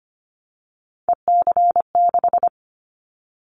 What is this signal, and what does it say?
Morse code, heard as a keyed tone, about 25 wpm, 700 Hz. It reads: EC6
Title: Callsign Prefixes for Top 100 DXCC - 25wpm